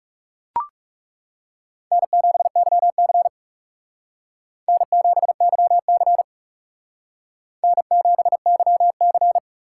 Morse code, heard as a keyed tone, about 34 wpm, 700 Hz. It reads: N7YC N7YC N7YC
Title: Speed Racing - USA Call Signs 30wpm